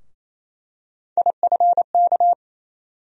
Morse code, heard as a keyed tone, about 28 wpm, 700 Hz. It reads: IFK
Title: USA Call Sign Postfixes - 3 Letters 28wpm